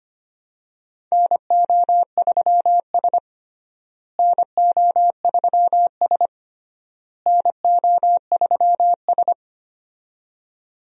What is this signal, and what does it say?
Morse code, heard as a keyed tone, about 25 wpm, 700 Hz. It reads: NO3S NO3S NO3S
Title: USA Call Signs - 4 Characters - Repeated 3x - 25wpm